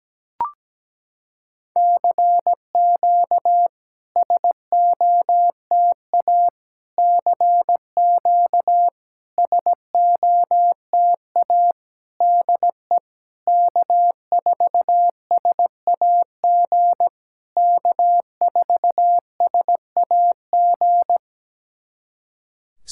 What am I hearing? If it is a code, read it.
Morse code, heard as a keyed tone, about 17 wpm, 700 Hz. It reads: CQ SOTA CQ SOTA DE K4SAG K4SAG
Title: SOTA QSO Practice 17wpm